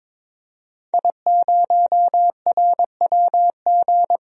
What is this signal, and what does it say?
Morse code, heard as a keyed tone, about 23 wpm, 700 Hz. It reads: I0RWG
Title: DXCC-100 Call Signs - 5 Characters - Repeated 2x - 22wpm